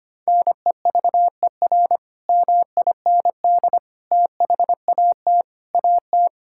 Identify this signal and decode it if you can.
Morse code, heard as a keyed tone, about 25 wpm, 700 Hz. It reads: NEVER MIND THAT AT
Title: Sentences from Top 1000 Words - Review Part 5 - 25wpm